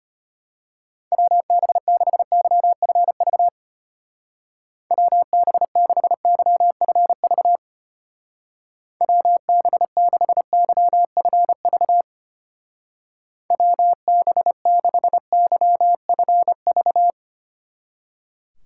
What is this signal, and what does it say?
Morse code, heard as a keyed tone, about 31 wpm, 700 Hz. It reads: WB6YFV WB6YFV WB6YFV WB6YFV